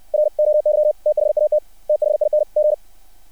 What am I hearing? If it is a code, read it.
Morse code, heard as a keyed tone, about 60 wpm, 600 Hz. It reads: ROY THAT THING
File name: archived / CRSnet